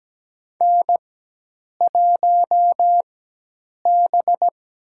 Morse code, heard as a keyed tone, about 17 wpm, spaced as for 8 wpm, 700 Hz. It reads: N1B
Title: USA Call Signs - 4x Character Spacing - 17wpm